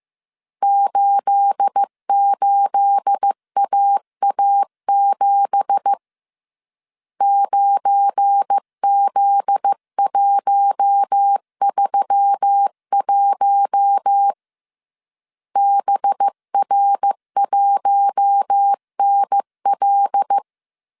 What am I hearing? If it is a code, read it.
Morse code, heard as a keyed tone, about 15 wpm, 800 Hz. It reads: 88AA7 9Z131 BR1NL
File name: CW / DET / 1 9 1500C S A 14wpm 3222